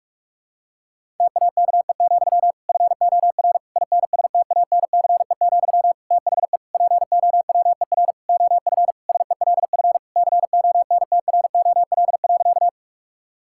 Morse code, heard as a keyed tone, about 46 wpm, 700 Hz. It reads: TAKE, FOR INSTANCE, THE POWER OF SELF CONTROL.
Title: Sentences from Top 1000 Words - Review Part 5 - 45wpm